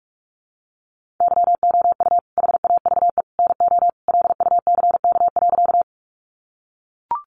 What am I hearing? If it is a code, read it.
Morse code, heard as a keyed tone, about 44 wpm, 700 Hz. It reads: YOU HAVE NO LUCK.
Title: Sentences from Top 1500 Words 45wpm